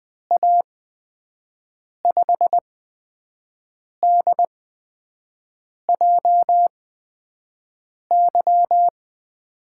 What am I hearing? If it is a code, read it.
Morse code, heard as a keyed tone, about 20 wpm, spaced as for 5 wpm, 700 Hz. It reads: A5DJY
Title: USA Call Signs - 6 Characters - 8x Character Spacing - 20wpm